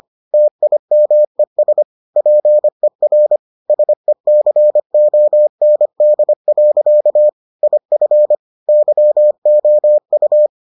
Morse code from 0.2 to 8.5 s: TIMES PER SECOND. IF

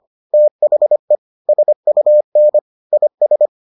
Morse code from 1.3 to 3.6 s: SUN IS

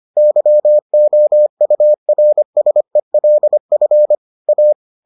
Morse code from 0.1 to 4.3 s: YOURSELF

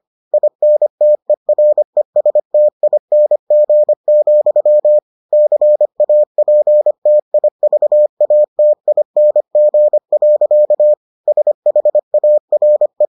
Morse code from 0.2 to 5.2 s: INTERESTING,

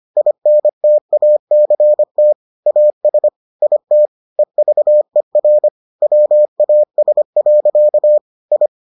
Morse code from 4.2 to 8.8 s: EVER WAS. I

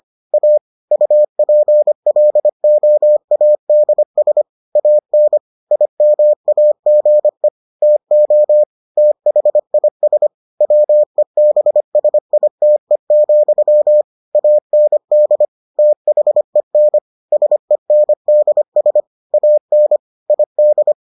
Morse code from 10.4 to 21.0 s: WEBSITE, AND THEN SENDS AN ID

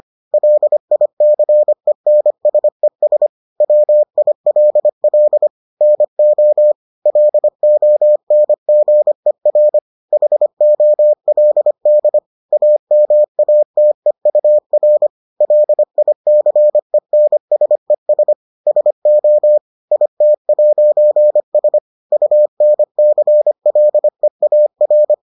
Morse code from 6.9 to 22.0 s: LONGER HOLD AMATEUR LICENSES SO IT'S

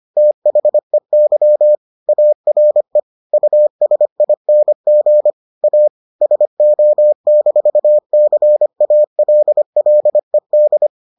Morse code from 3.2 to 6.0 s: USING A

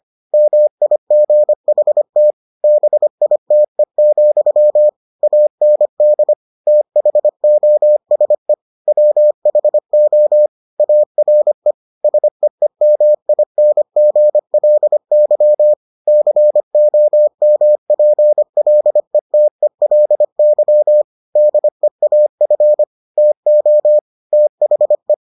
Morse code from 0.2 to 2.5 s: MIGHT

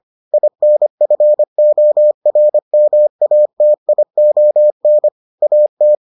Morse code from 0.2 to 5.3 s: INFORMATION

